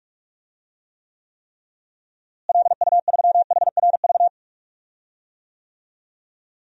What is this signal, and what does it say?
Morse code, heard as a keyed tone, about 46 wpm, 700 Hz. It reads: LU3HRV
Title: DXCC-100 Call Signs - 6 Characters - Repeated 3x - 45wpm